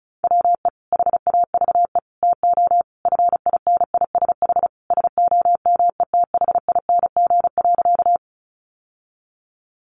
Morse code, heard as a keyed tone, about 34 wpm, 700 Hz. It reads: WE HAVE TO FINISH SOMETHING.
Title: Sentences from Top 1200 Words 35wpm